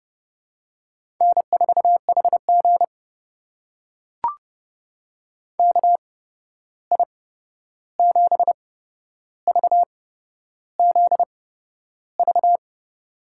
Morse code, heard as a keyed tone, about 30 wpm, spaced as for 11 wpm, 700 Hz. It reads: N4HG KI7VZV
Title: USA Call Signs - 8x Character Spacing - 30wpm